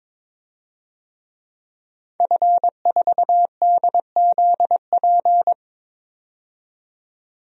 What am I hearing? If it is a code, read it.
Morse code, heard as a keyed tone, about 22 wpm, 700 Hz. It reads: F4DZP
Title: DXCC-100 Call Signs - 5 Characters - 22wpm